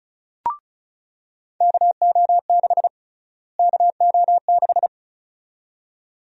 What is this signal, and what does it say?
Morse code, heard as a keyed tone, about 36 wpm, 700 Hz. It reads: KO6 KO6
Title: USA Call Sign Prefixes - Repeated 2x - 35wpm